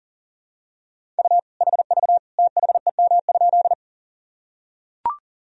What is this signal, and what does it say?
Morse code, heard as a keyed tone, about 40 wpm, 700 Hz. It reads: U HV THEM?
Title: Sentences from Top 100 Words and Abbreviations 40wpm